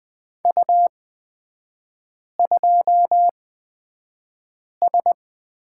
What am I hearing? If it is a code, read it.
Morse code, heard as a keyed tone, about 20 wpm, spaced as for 5 wpm, 700 Hz. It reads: U2S
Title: 255 Lesson 41e - Period - Three Characters 20wpm